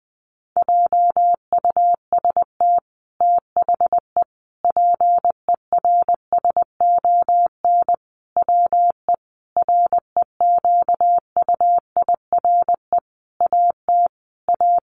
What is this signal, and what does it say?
Morse code, heard as a keyed tone, about 20 wpm, 700 Hz. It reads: JUST THE PERSON WE REQUIRE AT A
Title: Sentences from Top 900 Words 20wpm